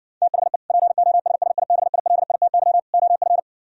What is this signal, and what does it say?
Morse code, heard as a keyed tone, about 65 wpm, 700 Hz. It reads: THE POSSIBILITY OF